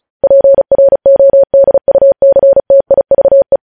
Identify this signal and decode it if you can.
Morse code, heard as a keyed tone, about 36 wpm, 550 Hz. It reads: PRODUCTIVE
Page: News headlines in Morse code at 35 wpm